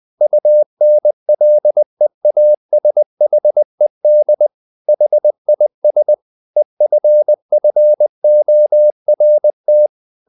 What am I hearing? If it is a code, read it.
Morse code, heard as a keyed tone, about 20 wpm, 600 Hz. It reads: UNLEASHED HIS EFFORT
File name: the old man and the sea part 2 20wpm